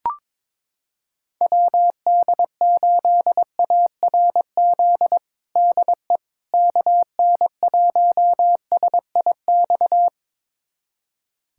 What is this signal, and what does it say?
Morse code, heard as a keyed tone, about 22 wpm, 700 Hz. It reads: WD8ARZ DE KN1SIX
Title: USA Callsign DE Callsign - 6 Characters - 22wpm